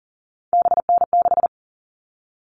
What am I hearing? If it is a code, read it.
Morse code, heard as a keyed tone, about 40 wpm, 700 Hz. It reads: BN6
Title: All Cut Numbers - 3-Digit 40wpm